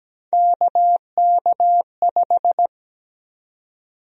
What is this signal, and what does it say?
Morse code, heard as a keyed tone, about 17 wpm, 700 Hz. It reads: KK5
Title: USA Call Sign Prefixes - Repeated 2x - 17wpm